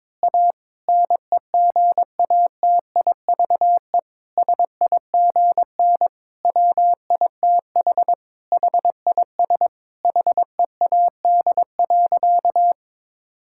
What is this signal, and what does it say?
Morse code, heard as a keyed tone, about 22 wpm, 700 Hz. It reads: A NEGATIVE SIGN WITH HIS HEAD.
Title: Sentences from Top 1000 Words - Review Part 6 - 22wpm